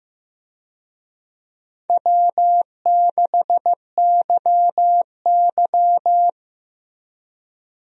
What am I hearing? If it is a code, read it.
Morse code, heard as a keyed tone, about 15 wpm, 700 Hz. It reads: W6YY
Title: USA Call Signs - 4 Characters - Repeated 2x - 15wpm